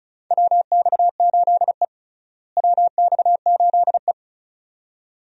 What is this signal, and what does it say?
Morse code, heard as a keyed tone, about 36 wpm, 700 Hz. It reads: WX8E WX8E
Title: USA Call Signs - 4 Characters - Repeated 2x - 35wpm